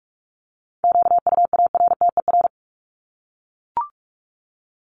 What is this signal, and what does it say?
Morse code, heard as a keyed tone, about 44 wpm, 700 Hz. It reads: QUARTER
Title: Top 1001-1250 Words 45wpm